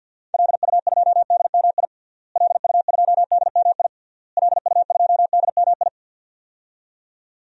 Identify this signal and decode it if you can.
Morse code, heard as a keyed tone, about 48 wpm, 700 Hz. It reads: LU2DMI LU2DMI LU2DMI
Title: DXCC-100 Call Signs - 6 Characters - Repeated 3x - 50wpm